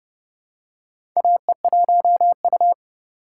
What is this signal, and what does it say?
Morse code, heard as a keyed tone, about 30 wpm, 700 Hz. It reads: AE1U
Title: USA Call Signs - 4x Character Spacing - 30wpm